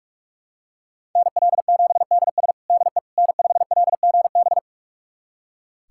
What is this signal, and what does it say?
Morse code, heard as a keyed tone, about 41 wpm, 700 Hz. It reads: NL7DS DE N5LGB
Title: USA Callsign DE Callsign - 5 Characters - 45wpm